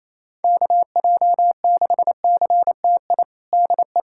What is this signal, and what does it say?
Morse code, heard as a keyed tone, about 27 wpm, 700 Hz. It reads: KJ6CTI DE
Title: USA Callsign DE Callsign - 6 Characters - 28wpm